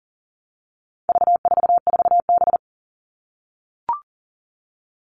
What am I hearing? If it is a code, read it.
Morse code, heard as a keyed tone, about 38 wpm, 700 Hz. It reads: V44B
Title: All Cut Numbers - 4-Digit 40wpm